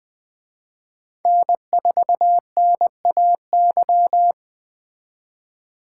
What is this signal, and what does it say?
Morse code, heard as a keyed tone, about 20 wpm, 700 Hz. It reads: N4NAY